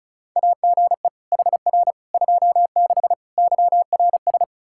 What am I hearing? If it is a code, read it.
Morse code, heard as a keyed tone, about 36 wpm, 700 Hz. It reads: AGE HR 26 YRS
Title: Practice QSO - Exchange 3 - More Info - 35wpm